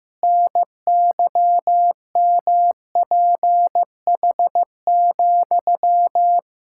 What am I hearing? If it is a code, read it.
Morse code, heard as a keyed tone, about 15 wpm, 700 Hz. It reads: NYMPH,
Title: Pangram Challenge v2 - 15wpm